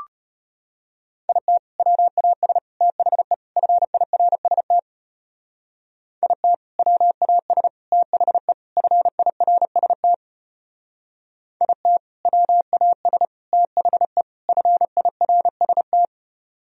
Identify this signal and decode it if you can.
Morse code, heard as a keyed tone, about 31 wpm, 690 Hz. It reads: IT WAS THE FIRST IT WAS THE FIRST IT WAS THE FIRST
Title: Speed-Racing - Sets of 4 Words 25wpm